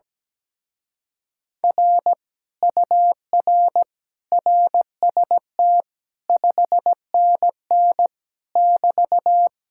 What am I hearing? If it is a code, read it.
Morse code, heard as a keyed tone, about 17 wpm, 700 Hz. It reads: R UR RST 5NN <BT>
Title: Speed-Racing Common QSO Phrases - 17wpm